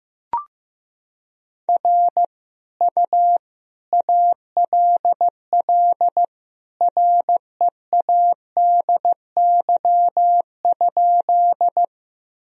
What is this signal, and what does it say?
Morse code, heard as a keyed tone, about 15 wpm, 700 Hz. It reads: R U ALL READY?
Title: Sentences from Top 400 Words and Abbreviations 15wpm